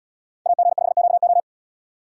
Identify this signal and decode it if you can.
Morse code, heard as a keyed tone, about 78 wpm, 700 Hz. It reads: AG5QO